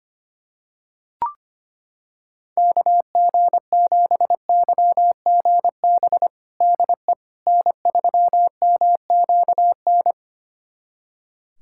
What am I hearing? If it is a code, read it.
Morse code, heard as a keyed tone, about 25 wpm, 700 Hz. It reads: KG7YGB DE N3MQN
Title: USA Callsign DE Callsign 25wpm